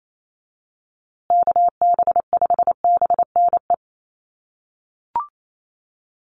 Morse code, heard as a keyed tone, about 28 wpm, 690 Hz. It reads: KB5BNE
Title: USA Call Signs Repeated 3x 28wpm